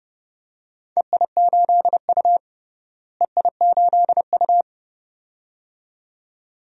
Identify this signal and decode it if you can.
Morse code, heard as a keyed tone, about 31 wpm, 700 Hz. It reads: EI8U EI8U
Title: DXCC-100 Call Signs - 4 Characters - Repeated 2x - 30wpm